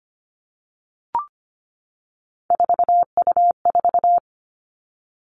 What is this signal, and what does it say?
Morse code, heard as a keyed tone, about 25 wpm, 700 Hz. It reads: 4U4
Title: Three Challenging Letter-Number Combinations 25wpm